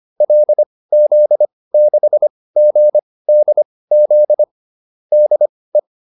Morse code, heard as a keyed tone, about 25 wpm, spaced as for 18 wpm, 600 Hz. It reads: LZ6GDZ DE